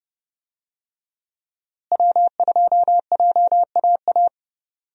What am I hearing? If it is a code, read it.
Morse code, heard as a keyed tone, about 31 wpm, 700 Hz. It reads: W2JAA
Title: USA Call Signs - 5 Characters - Repeated 2x - 30wpm